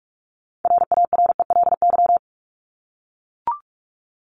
Morse code, heard as a keyed tone, about 44 wpm, 700 Hz. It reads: RARELY
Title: Top 1251-1500 Words 45wpm